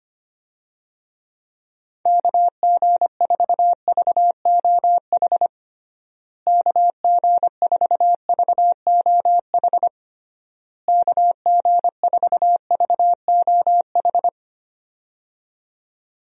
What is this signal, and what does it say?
Morse code, heard as a keyed tone, about 25 wpm, 700 Hz. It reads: KG4VOH KG4VOH KG4VOH
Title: USA Call Signs - 6 Characters - Repeated 3x - 25wpm